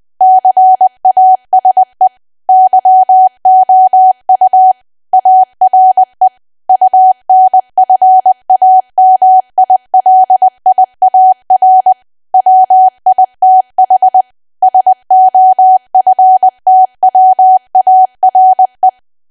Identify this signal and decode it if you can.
Morse code, heard as a keyed tone, about 20 wpm, 750 Hz. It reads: CASE YOU ARE UNFAMILIAR WITH SOFTWARE
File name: cwa / W1AW-Level3 / 20WPM